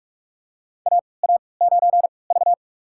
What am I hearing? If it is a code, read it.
Morse code, heard as a keyed tone, about 41 wpm, spaced as for 25 wpm, 690 Hz. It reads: AA9V
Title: USA Call Signs - 4 Characters - 3x Character Spacing - 45wpm